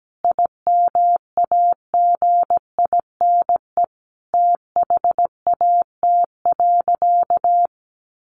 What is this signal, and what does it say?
Morse code, heard as a keyed tone, about 17 wpm, 700 Hz. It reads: IMAGINE THAT.
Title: Sentences from Top 1400 Words 17wpm